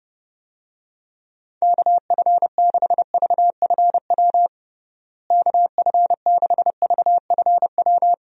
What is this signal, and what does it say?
Morse code, heard as a keyed tone, about 30 wpm, 700 Hz. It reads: KF6VFW KF6VFW
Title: USA Call Signs - 6 Characters - Repeated 3x - 30wpm